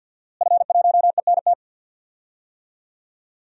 Morse code, heard as a keyed tone, about 49 wpm, 700 Hz. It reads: F1ENT
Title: DXCC-100 Call Signs - 5 Characters - Repeated 2x - 50wpm